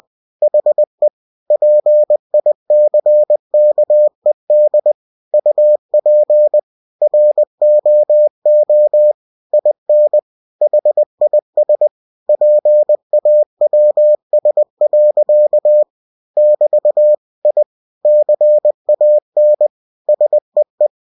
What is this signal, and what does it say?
Morse code, heard as a keyed tone, about 20 wpm, 600 Hz. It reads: HE PICKED UP ROO IN HIS PAWS. <BT> I CAN SEE
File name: cw-books / winnie-the-pooh / winnie-20wpm-page-0069